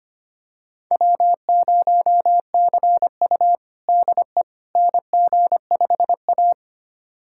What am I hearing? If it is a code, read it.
Morse code, heard as a keyed tone, about 25 wpm, 700 Hz. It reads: W0CU DE NG5A
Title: USA Callsign DE Callsign - 4 Characters - 25wpm